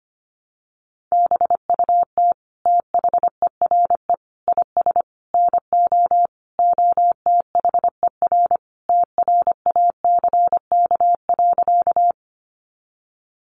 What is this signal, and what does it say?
Morse code, heard as a keyed tone, about 25 wpm, 700 Hz. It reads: BUT THERE IS NO OTHER TRACK.
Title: Sentences from Top 1100 Words 25wpm